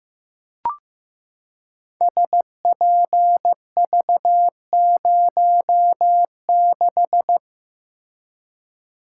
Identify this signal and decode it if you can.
Morse code, heard as a keyed tone, about 15 wpm, 700 Hz. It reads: SPV06
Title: Code Groups - 5 Letter:Numbers - 15wpm